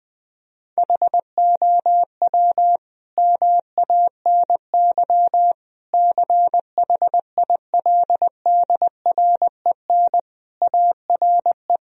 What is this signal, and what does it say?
Morse code, heard as a keyed tone, about 20 wpm, 700 Hz. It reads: HOW MANY CHILDREN ARE
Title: Sentences from Top 1000 Words - Review Part 4 - 20wpm